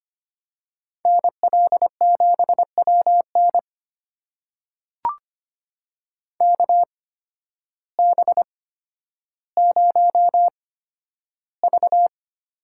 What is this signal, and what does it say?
Morse code, heard as a keyed tone, about 25 wpm, spaced as for 12 wpm, 700 Hz. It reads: NL7WN KB0V
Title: USA Call Signs - 8x Character Spacing - 25wpm